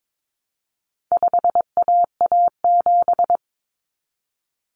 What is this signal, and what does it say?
Morse code, heard as a keyed tone, about 22 wpm, 700 Hz. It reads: 5AA7
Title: Common Cut Numbers - 4-Digit 22wpm